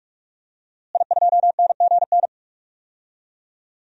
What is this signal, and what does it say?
Morse code, heard as a keyed tone, about 43 wpm, 700 Hz. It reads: I2NGN
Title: DXCC-100 Call Signs - 5 Characters - Repeated 2x - 45wpm